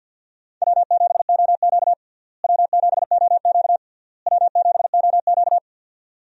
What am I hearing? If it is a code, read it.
Morse code, heard as a keyed tone, about 50 wpm, 700 Hz. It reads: W7OQ W7OQ W7OQ
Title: USA Call Signs - 4 Characters - Repeated 3x - 50wpm